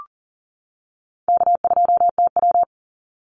Morse code, heard as a keyed tone, about 40 wpm, 700 Hz. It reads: K2TW
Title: Easy Call Signs - 40wpm